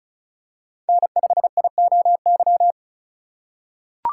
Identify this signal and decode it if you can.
Morse code, heard as a keyed tone, about 36 wpm, 700 Hz. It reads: N5IOY
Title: USA Call Signs - 5 Characters - 2x Character Spacing - 35wpm